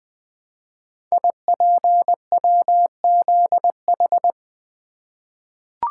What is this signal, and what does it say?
Morse code, heard as a keyed tone, about 20 wpm, 700 Hz. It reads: IPWZH
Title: Code Groups - 5 Letters - 20wpm